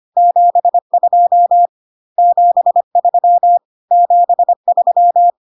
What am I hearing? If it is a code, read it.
Morse code, heard as a keyed tone, about 25 wpm, 700 Hz. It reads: 72 73 73
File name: CW abbreviations alphabetical repeated 3 times 25 25wpm ik1hge